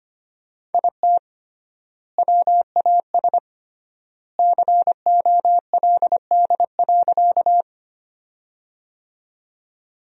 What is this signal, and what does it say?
Morse code, heard as a keyed tone, about 25 wpm, 700 Hz. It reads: IT WAS COLD.
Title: 3x Word-Spacing - Sentences from Top 400 Words - 25wpm